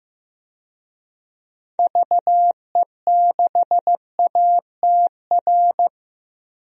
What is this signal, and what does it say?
Morse code, heard as a keyed tone, about 15 wpm, 700 Hz. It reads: VE6ATR
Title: DXCC-100 Call Signs - 6 Characters - 15wpm